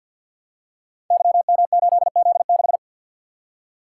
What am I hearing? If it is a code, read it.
Morse code, heard as a keyed tone, about 51 wpm, 700 Hz. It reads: YM8Z6
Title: Code Groups - 5 Letter:Numbers - 50wpm